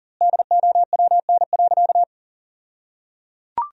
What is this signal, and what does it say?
Morse code, heard as a keyed tone, about 37 wpm, 700 Hz. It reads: DOWN.
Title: 2x Word-Spacing - Sentences from Top 500 Words - 40wpm